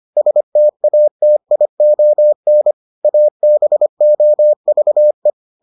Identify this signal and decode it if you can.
Morse code, heard as a keyed tone, about 25 wpm, 600 Hz. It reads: STATION ABOVE